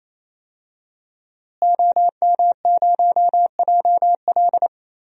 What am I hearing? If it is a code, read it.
Morse code, heard as a keyed tone, about 28 wpm, 700 Hz. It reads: OM0JL
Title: DXCC-100 Call Signs - 5 Characters - Repeated 3x - 28wpm